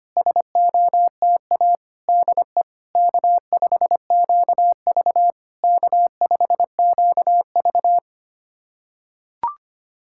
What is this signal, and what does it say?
Morse code, heard as a keyed tone, about 25 wpm, 690 Hz. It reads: SOTA DE K5QV K5QV
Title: SOTA QSO Practice 25wpm